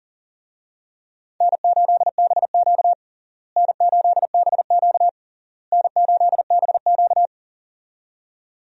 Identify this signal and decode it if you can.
Morse code, heard as a keyed tone, about 37 wpm, 700 Hz. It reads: N8BQ N8BQ N8BQ
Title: USA Call Signs - 4 Characters - Repeated 3x - 40wpm